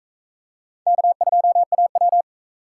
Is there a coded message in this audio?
Yes: Morse code from 0.9 to 2.2 s, heard as a keyed tone, about 44 wpm, 700 Hz. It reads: K2AW